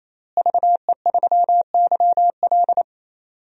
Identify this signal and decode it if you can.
Morse code, heard as a keyed tone, about 28 wpm, 700 Hz. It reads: VE3YL
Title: DXCC-100 Call Signs - 5 Characters - Repeated 3x - 28wpm